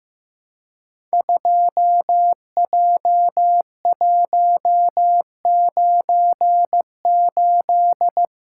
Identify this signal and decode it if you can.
Morse code, heard as a keyed tone, about 15 wpm, 700 Hz. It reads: 2J198
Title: Code Groups - 5 Letter:Numbers - 15wpm